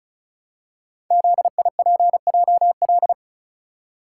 Morse code, heard as a keyed tone, about 36 wpm, 700 Hz. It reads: ZIPJL
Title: Code Groups - 5 Letter:Numbers - 35wpm